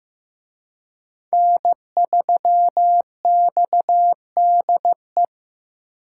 Morse code, heard as a keyed tone, about 18 wpm, 700 Hz. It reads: N3XDE
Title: Speed Racing - USA 1x3 Call Signs 15wpm